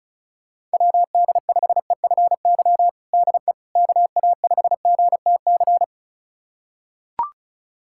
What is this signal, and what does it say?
Morse code, heard as a keyed tone, about 36 wpm, 700 Hz. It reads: WD5EFY DE KA5GTC
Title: USA Callsign DE Callsign - 6 Characters - 35wpm